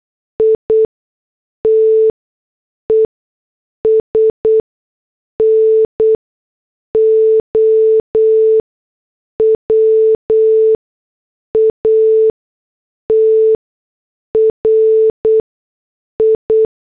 Morse code, heard as a keyed tone, about 8 wpm, spaced as for 6 wpm, 440 Hz. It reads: ITES NOW A TRI